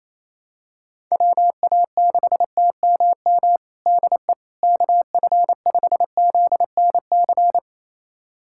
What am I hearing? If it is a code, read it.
Morse code, heard as a keyed tone, about 28 wpm, 700 Hz. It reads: WA6TMM DE KF5ZNC
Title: USA Callsign DE Callsign - 6 Characters - 28wpm